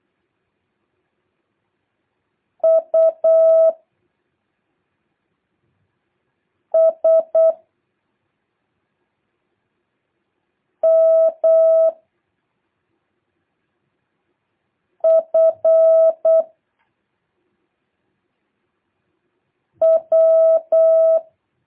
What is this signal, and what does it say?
Morse code, heard as a keyed tone, about 8 wpm, 660 Hz. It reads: U S M F W
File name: jh8cbh / 2023cw